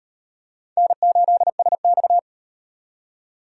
Morse code, heard as a keyed tone, about 35 wpm, 700 Hz. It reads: N8SX